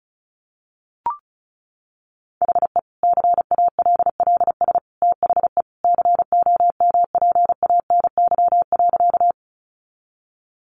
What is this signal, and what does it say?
Morse code, heard as a keyed tone, about 34 wpm, 700 Hz. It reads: HE CALLS THE COMPANY.
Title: Additional Easy Phrases 35wpm